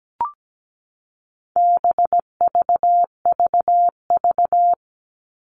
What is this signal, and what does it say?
Morse code, heard as a keyed tone, about 17 wpm, 700 Hz. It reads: BVVV
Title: All Cut Numbers - 4-Digit 17wpm